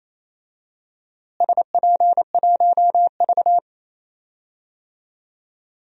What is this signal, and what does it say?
Morse code, heard as a keyed tone, about 28 wpm, 700 Hz. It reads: SP1V
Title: DXCC-100 Call Signs - 4 Characters - Repeated 2x - 28wpm